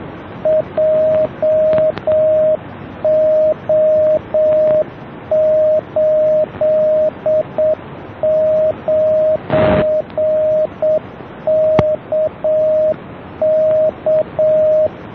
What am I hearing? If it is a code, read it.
Morse code, heard as a keyed tone, about 7 wpm, 620 Hz, with radio noise. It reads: JO89KK